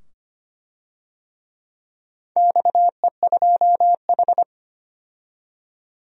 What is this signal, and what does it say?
Morse code, heard as a keyed tone, about 25 wpm, 700 Hz. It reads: XE2H
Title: DXCC-100 Call Signs - 4 Characters - 25wpm